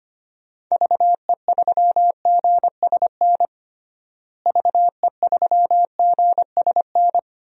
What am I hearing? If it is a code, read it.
Morse code, heard as a keyed tone, about 25 wpm, 700 Hz. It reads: VE3GSN VE3GSN
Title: DXCC-100 Call Signs - 6 Characters - Repeated 2x - 25wpm